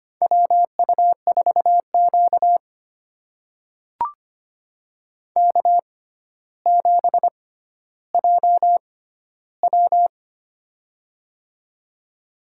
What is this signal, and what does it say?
Morse code, heard as a keyed tone, about 25 wpm, spaced as for 13 wpm, 700 Hz. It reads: WU4Q K7JW
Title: USA Call Signs - 4 Characters - 6x Character Spacing - 25wpm